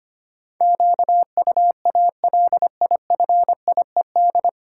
Morse code, heard as a keyed tone, about 25 wpm, 700 Hz. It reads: QUALIFIED
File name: Nine Letter Words - 25wpm